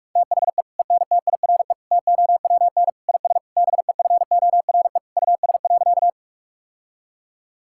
Morse code, heard as a keyed tone, about 46 wpm, 700 Hz. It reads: THE ENTIRE TOWN IS BEFORE US.
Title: Sentences from Top 1000 Words - Review Part 5 - 45wpm